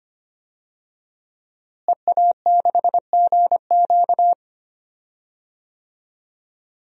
Morse code, heard as a keyed tone, about 25 wpm, 700 Hz. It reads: EA6GQ
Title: DXCC-100 Call Signs - 5 Characters - Repeated 2x - 25wpm